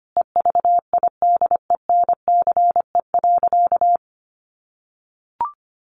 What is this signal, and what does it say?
Morse code, heard as a keyed tone, about 25 wpm, 700 Hz. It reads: EVIDENCE.
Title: Sentences from Top 1300 Words 25wpm